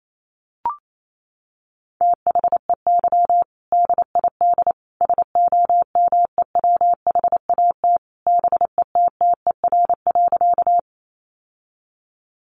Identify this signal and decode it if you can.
Morse code, heard as a keyed tone, about 28 wpm, 700 Hz. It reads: THEY DID SOMEWHAT BETTER.
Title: Sentences from Top 1300 Words 28wpm